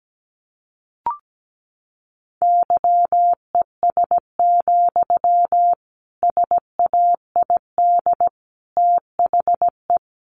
Morse code, heard as a keyed tone, about 17 wpm, 700 Hz. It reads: YES, SAID THE
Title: A Sound Of Thunder 17wpm